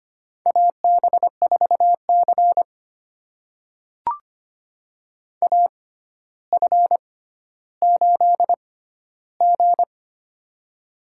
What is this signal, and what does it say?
Morse code, heard as a keyed tone, about 25 wpm, spaced as for 13 wpm, 700 Hz. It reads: AB4C AF8G
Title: USA Call Signs - 4 Characters - 6x Character Spacing - 25wpm